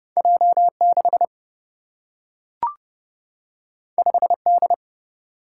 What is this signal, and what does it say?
Morse code, heard as a keyed tone, about 29 wpm, 700 Hz. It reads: J 6 5 D
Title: Single-Letter Single-Number 30wpm